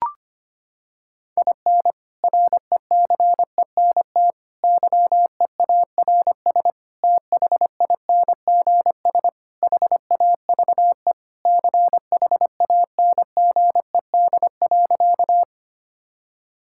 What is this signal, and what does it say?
Morse code, heard as a keyed tone, about 25 wpm, 700 Hz. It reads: IN RECENT YEARS THINGS HAVE CHANGED.
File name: practice / 25wpm / Sentences with 3-Word Phrases - 25wpm